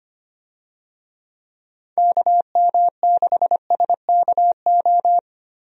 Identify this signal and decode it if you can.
Morse code, heard as a keyed tone, about 25 wpm, 700 Hz. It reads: KM6SKO
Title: USA Call Signs - 6 Characters - 25wpm